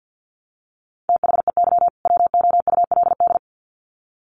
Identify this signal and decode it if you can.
Morse code, heard as a keyed tone, about 50 wpm, 700 Hz. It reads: THEY WOULD
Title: Sets of 2 Words 50wpm